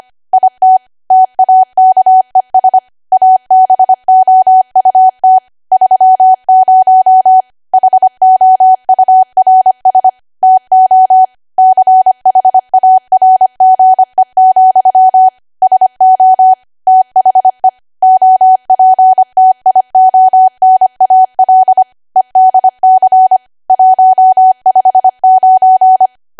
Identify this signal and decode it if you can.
Morse code, heard as a keyed tone, about 25 wpm, 750 Hz. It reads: IT TAKES ABOUT 30 HOURS TO CHARGE, SO THE OPTIONAL EDC 159
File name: Archive / 25 WPM / 25WPM